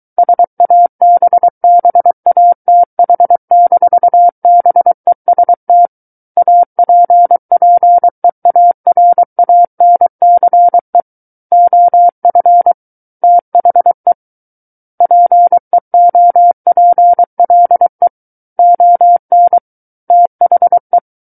Morse code, heard as a keyed tone, about 23 wpm, 700 Hz. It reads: SABBATH-BEST APPEARANCE OF THE PEOPLE ON THE
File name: Book One / 14-Chapter 14 - 23 WPM